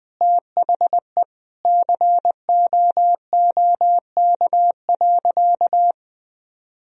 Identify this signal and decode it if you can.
Morse code, heard as a keyed tone, about 21 wpm, 700 Hz. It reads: THE COOK.
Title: Sentences from Top 1000 Words - Review Part 4 - 20wpm